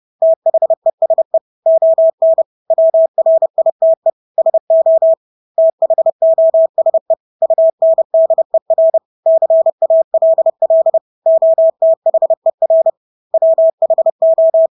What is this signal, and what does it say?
Morse code, heard as a keyed tone, about 30 wpm, 650 Hz. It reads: THESE ON WRITE SO THOSE UNDER CALL OTHER WHO
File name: Engelske ord 30-30 wpm